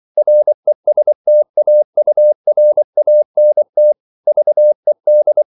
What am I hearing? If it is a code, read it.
Morse code, heard as a keyed tone, about 24 wpm, 600 Hz. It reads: RESTAURANT VED
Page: Ferie 24 wpm